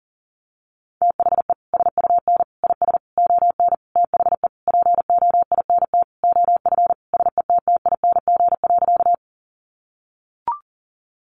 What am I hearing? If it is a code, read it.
Morse code, heard as a keyed tone, about 40 wpm, 700 Hz. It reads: THE SUN IS ON THE POINT OF SETTING.
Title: Sentences from Top 1300 Words 40wpm